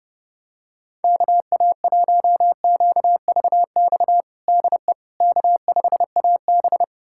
Morse code, heard as a keyed tone, about 30 wpm, 700 Hz. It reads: KA1QVX DE K5AB
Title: USA Callsign DE Callsign 30wpm